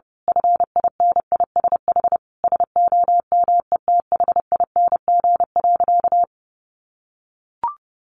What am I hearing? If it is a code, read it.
Morse code, heard as a keyed tone, about 30 wpm, 700 Hz. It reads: FINISH SOMETHING.
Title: Sentences from Top 1200 Words 30wpm